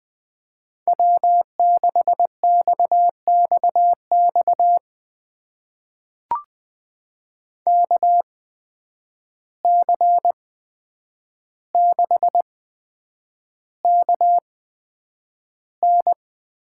Morse code, heard as a keyed tone, about 20 wpm, spaced as for 9 wpm, 700 Hz. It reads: W6XXX KC6KN
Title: USA Call Signs - 8x Character Spacing - 20wpm